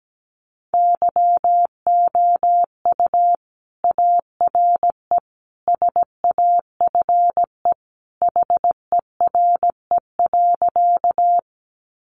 Morse code, heard as a keyed tone, about 17 wpm, 700 Hz. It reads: YOU ARE SAFE HERE.
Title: Sentences from Top 900 Words 17wpm